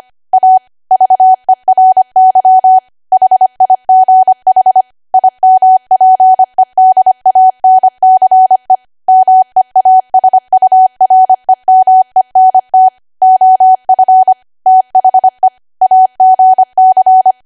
Morse code, heard as a keyed tone, about 25 wpm, 750 Hz. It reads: A VERY HIGH IMPEDANCE MEASUREMENT OF THE AGC
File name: Archive / 25 WPM / 25WPM